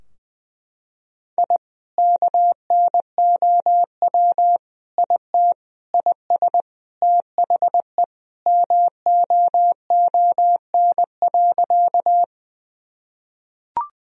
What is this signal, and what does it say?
Morse code, heard as a keyed tone, about 20 wpm, 700 Hz. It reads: I KNOW IT IS THE MOON.
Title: Sentences from Top 1000 Words - Review Part 3 - 20wpm